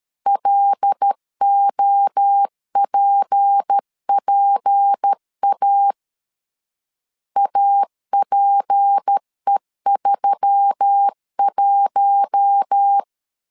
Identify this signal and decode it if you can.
Morse code, heard as a keyed tone, about 13 wpm, 800 Hz. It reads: LOPPA APE31
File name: LEZIONI / 6 PKX0 / 6 P 12wpm 2033